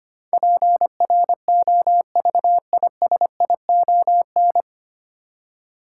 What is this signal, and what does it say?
Morse code, heard as a keyed tone, about 25 wpm, 700 Hz. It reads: PROVISION
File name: Nine Letter Words - 25wpm